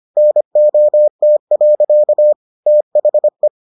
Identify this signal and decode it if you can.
Morse code, heard as a keyed tone, about 25 wpm, 600 Hz. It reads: NOT. THE